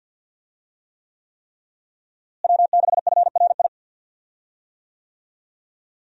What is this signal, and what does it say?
Morse code, heard as a keyed tone, about 51 wpm, 700 Hz. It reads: W6FRI
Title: USA Call Signs - 5 Characters - Repeated 3x - 50wpm